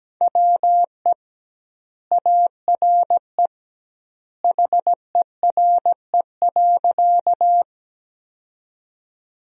Morse code, heard as a keyed tone, about 17 wpm, 700 Hz. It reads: WE ARE HERE.
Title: 2x Word-Spacing - Sentences from Top 200 Words - 17wpm